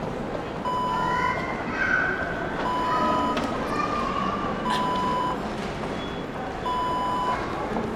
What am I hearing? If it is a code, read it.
There is no code.